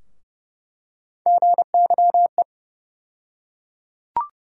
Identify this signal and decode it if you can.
Morse code, heard as a keyed tone, about 30 wpm, 700 Hz. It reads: GYE
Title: USA Call Sign Postfixes - 3 Letters 30wpm